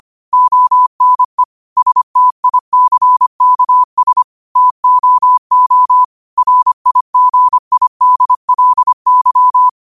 Morse code, heard as a keyed tone, about 25 wpm, 1000 Hz. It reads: ONE STICKS TOO RIGIDLY
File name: qotdfiles / 25wpm